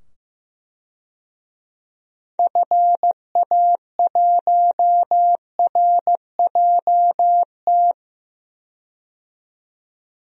Morse code, heard as a keyed tone, about 15 wpm, 700 Hz. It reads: FA1RJT
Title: DXCC-100 Call Signs - 6 Characters - 15wpm